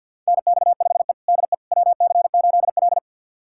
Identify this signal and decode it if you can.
Morse code, heard as a keyed tone, about 50 wpm, 700 Hz. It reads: NX5E DE WC8L
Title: USA Callsign DE Callsign - 4 Characters - 50wpm